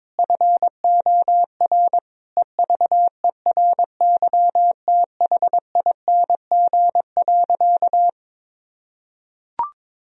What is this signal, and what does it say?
Morse code, heard as a keyed tone, about 22 wpm, 700 Hz. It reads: FOR EVERYTHING.
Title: Sentences from Top 1000 Words - Review Part 1 - 22wpm